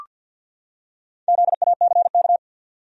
Morse code, heard as a keyed tone, about 47 wpm, 700 Hz. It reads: ZACK
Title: USA Common Names - Max 4 Letters - 50wpm